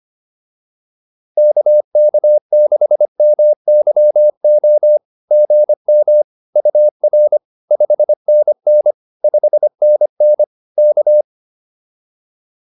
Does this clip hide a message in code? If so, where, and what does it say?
Yes: Morse code from 1.4 to 11.2 s, heard as a keyed tone, about 25 wpm, 600 Hz. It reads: KK6MYO GM UR 5NN 5NN K